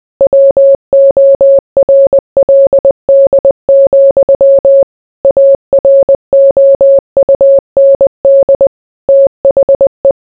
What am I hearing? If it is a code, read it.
Morse code, heard as a keyed tone, about 20 wpm, 550 Hz. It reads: WORLD, AROUND THE